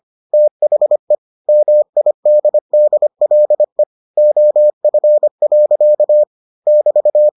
Morse code, heard as a keyed tone, about 25 wpm, 600 Hz. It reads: THE MIDDLE OF. <BT>